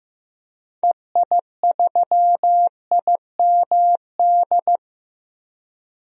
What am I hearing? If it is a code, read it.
Morse code, heard as a keyed tone, about 16 wpm, 700 Hz. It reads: EI3IMD
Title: DXCC-100 Call Signs - 6 Characters - Repeated 2x - 15wpm